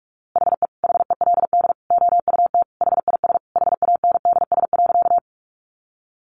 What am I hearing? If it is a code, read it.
Morse code, heard as a keyed tone, about 44 wpm, 700 Hz. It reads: HE HELD OUT HIS HANDS.
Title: A Sound Of Thunder 45wpm